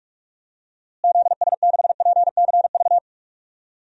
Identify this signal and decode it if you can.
Morse code, heard as a keyed tone, about 43 wpm, 700 Hz. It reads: ZS6PCV
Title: DXCC-100 Call Signs - 6 Characters - Repeated 2x - 45wpm